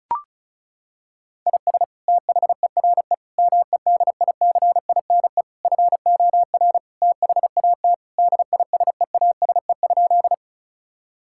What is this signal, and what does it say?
Morse code, heard as a keyed tone, about 36 wpm, 700 Hz. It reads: IS THERE MEDICINE FOR THAT DISEASE?
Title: Sentences from Top 1000 Words - Review Part 6 - 35wpm